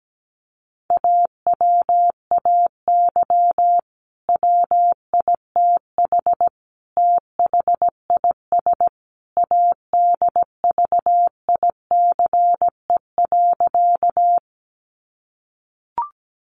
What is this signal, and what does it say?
Morse code, heard as a keyed tone, about 17 wpm, 700 Hz. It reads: AWAY WITH THIS ADVICE.
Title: Sentences from Top 1100 Words 17wpm